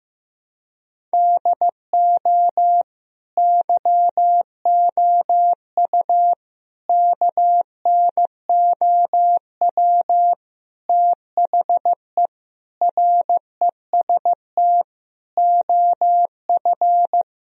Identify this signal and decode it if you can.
Morse code, heard as a keyed tone, about 15 wpm, 700 Hz. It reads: DO YOU KNOW THE REST OF
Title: Sentences from Top 1000 Words - Review Part 2 - 15wpm